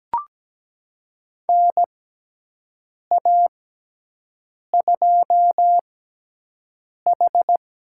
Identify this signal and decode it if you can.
Morse code, heard as a keyed tone, about 17 wpm, spaced as for 6 wpm, 700 Hz. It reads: NA2H
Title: USA Call Signs - 4 Characters - 6x Character Spacing - 17wpm